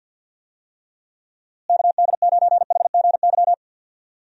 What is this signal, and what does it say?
Morse code, heard as a keyed tone, about 46 wpm, 700 Hz. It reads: KD9HGY
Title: USA Call Signs - 6 Characters - Repeated 2x - 50wpm